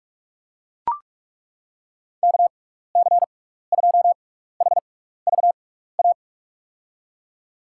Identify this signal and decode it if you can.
Morse code, heard as a keyed tone, about 41 wpm, spaced as for 15 wpm, 690 Hz. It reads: KC2HVA
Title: USA Call Signs - 6x Character Spacing - 45wpm